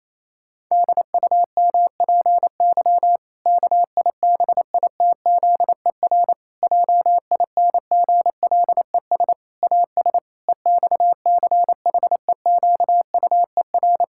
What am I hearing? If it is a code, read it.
Morse code, heard as a keyed tone, about 27 wpm, 700 Hz. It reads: DUMPY KIBITZER JINGLES AS EXCHEQUER
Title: Pangram Challenge v2 - 28wpm